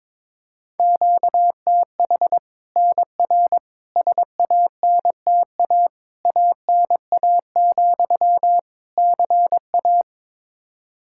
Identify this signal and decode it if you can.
Morse code, heard as a keyed tone, about 22 wpm, 700 Hz. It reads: QTH NR SANTA ANA, CA
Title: Practice QSO - Exchange 1 - Beginning - 22wpm